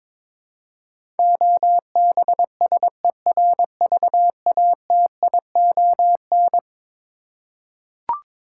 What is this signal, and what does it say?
Morse code, heard as a keyed tone, about 22 wpm, 700 Hz. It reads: OBSERVATION
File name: practice / 22wpm / Eleven Letter Words - 22wpm